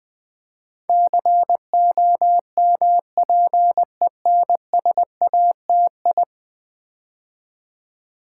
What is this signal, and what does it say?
Morse code, heard as a keyed tone, about 20 wpm, 700 Hz. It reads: COMPENSATI
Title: Build A Word - 5 Letters At A Time - 12-Letter Words - 20wpm